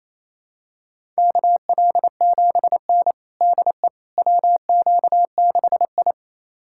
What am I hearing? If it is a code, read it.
Morse code, heard as a keyed tone, about 27 wpm, 700 Hz. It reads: KL7N DE WQ6I
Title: USA Callsign DE Callsign - 4 Characters - 28wpm